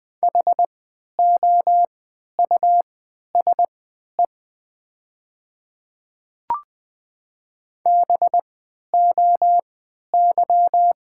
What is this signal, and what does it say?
Morse code, heard as a keyed tone, about 20 wpm, spaced as for 11 wpm, 700 Hz. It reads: HOUSE BOY
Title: 3x Character Spacing - Top 200 Words - 20wpm